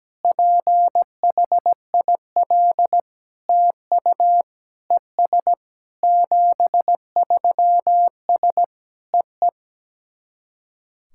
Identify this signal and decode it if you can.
Morse code, heard as a keyed tone, about 17 wpm, 700 Hz. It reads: PHIL TU ES 73S EE
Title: SOTA QSO Practice 17wpm